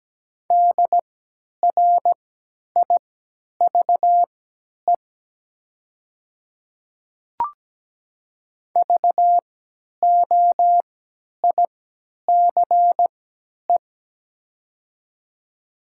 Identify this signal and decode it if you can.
Morse code, heard as a keyed tone, about 17 wpm, spaced as for 10 wpm, 700 Hz. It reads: DRIVE VOICE
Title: 3x Character Spacing - Top 500 Words - 17wpm